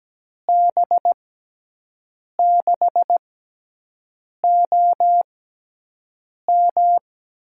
Morse code, heard as a keyed tone, about 17 wpm, spaced as for 6 wpm, 700 Hz. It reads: B6OM
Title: USA Call Signs - 5 Characters - 6x Character Spacing - 17wpm